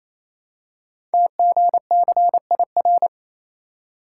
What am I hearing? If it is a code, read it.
Morse code, heard as a keyed tone, about 27 wpm, 700 Hz. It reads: TGCIR
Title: Code Groups - 5 Letters - 28wpm